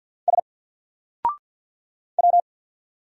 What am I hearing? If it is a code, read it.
Morse code, heard as a keyed tone, about 50 wpm, 700 Hz. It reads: S W
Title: Single Letter-Number 50wpm